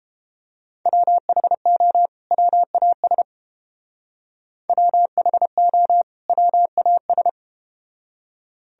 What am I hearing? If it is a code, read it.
Morse code, heard as a keyed tone, about 31 wpm, 700 Hz. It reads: WHO WAS WHO WAS